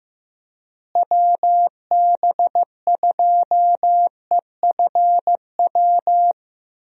Morse code, heard as a keyed tone, about 15 wpm, 700 Hz. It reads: WB2EFW
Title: USA Call Signs - 6 Characters - Repeated 3x - 15wpm